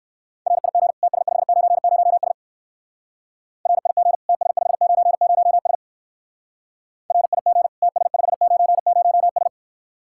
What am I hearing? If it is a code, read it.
Morse code, heard as a keyed tone, about 56 wpm, 700 Hz. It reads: RIG TS590S RIG TS590S RIG TS590S
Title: Speed-Racing Common QSO Phrases - 45wpm